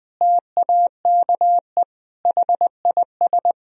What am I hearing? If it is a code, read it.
Morse code, heard as a keyed tone, about 20 wpm, 700 Hz. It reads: TAKE HIS
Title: Sentences from Top 1000 Words - Review Part 4 - 20wpm